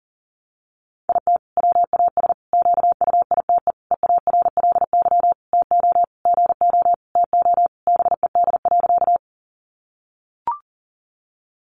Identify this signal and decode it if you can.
Morse code, heard as a keyed tone, about 37 wpm, 700 Hz. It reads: IT WAS QUITE EARLY TO GO TO BED.
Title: Sentences from Top 700 Words 40wpm